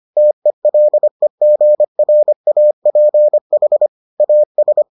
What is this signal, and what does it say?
Morse code, heard as a keyed tone, about 25 wpm, 600 Hz. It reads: TELEGRAPH AS